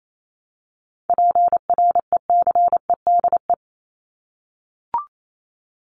Morse code, heard as a keyed tone, about 28 wpm, 700 Hz. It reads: PRECEDE
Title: ING Suffix - Top 100 Words with Base Words 28wpm